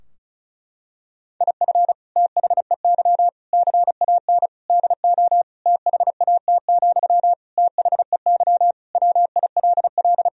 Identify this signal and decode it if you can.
Morse code, heard as a keyed tone, about 36 wpm, 700 Hz. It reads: IF THEY CAN DO THAT, THEY WILL
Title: Sentences from Top 1000 Words - Review Part 4 - 35wpm